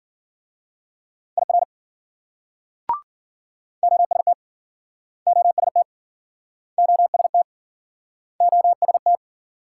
Morse code, heard as a keyed tone, about 52 wpm, 700 Hz. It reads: IR OST OST OST OST